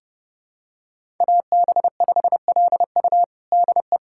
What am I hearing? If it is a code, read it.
Morse code, heard as a keyed tone, about 30 wpm, 700 Hz. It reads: AB5LU DE
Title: USA Callsign DE Callsign 30wpm